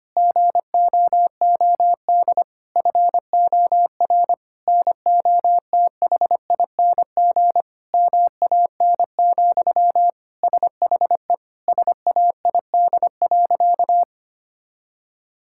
Morse code, heard as a keyed tone, about 25 wpm, 700 Hz. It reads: GOOD FOR NOTHING MAN, SHE SAID.
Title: Sentences from Top 1000 Words - Review Part 6 - 25wpm